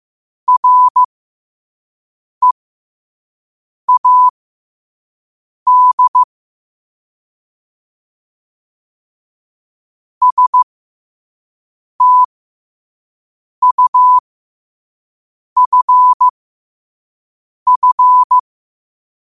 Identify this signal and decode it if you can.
Morse code, heard as a keyed tone, about 15 wpm, spaced as for 5 wpm, 1000 Hz. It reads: READ STUFF
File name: qotdfiles / 5wpm